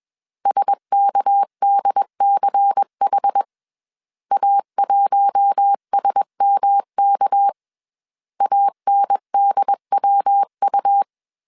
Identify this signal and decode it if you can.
Morse code, heard as a keyed tone, about 21 wpm, 800 Hz. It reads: SKDCH A1SMK ANDWU
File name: CW / DET / 1 6 1500C S A 20wpm 2048